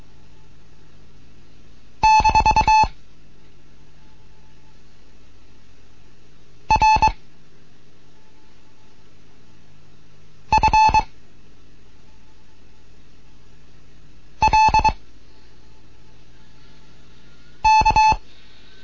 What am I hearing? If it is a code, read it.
Morse code, heard as a keyed tone, about 23 wpm, 870 Hz. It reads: - R F L K